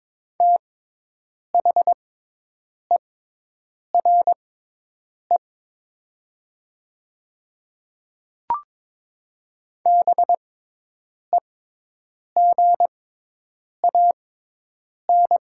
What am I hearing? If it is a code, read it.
Morse code, heard as a keyed tone, about 22 wpm, spaced as for 8 wpm, 700 Hz. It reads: THERE BEGAN
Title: 6x Character Spacing - Top 300 Words - 22wpm